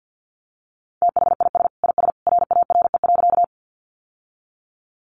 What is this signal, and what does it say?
Morse code, heard as a keyed tone, about 50 wpm, 700 Hz. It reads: THIS IS RARE.
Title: Sentences from Top 1100 Words 50wpm